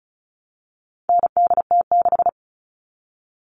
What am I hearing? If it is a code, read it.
Morse code, heard as a keyed tone, about 34 wpm, 700 Hz. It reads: NDT6
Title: All Cut Numbers - 4-Digit 35wpm